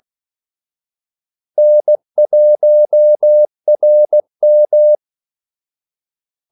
Morse code, heard as a keyed tone, about 16 wpm, 600 Hz. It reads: N1RM
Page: Kaldesignaler 16 wpm